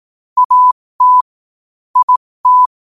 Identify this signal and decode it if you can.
Morse code, heard as a keyed tone, about 18 wpm, spaced as for 15 wpm, 1000 Hz. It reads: AT IT
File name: qotdfiles / 15wpm